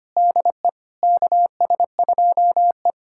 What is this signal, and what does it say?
Morse code, heard as a keyed tone, about 25 wpm, 700 Hz. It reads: DE KS2E
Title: USA Callsign DE Callsign - 4 Characters - 25wpm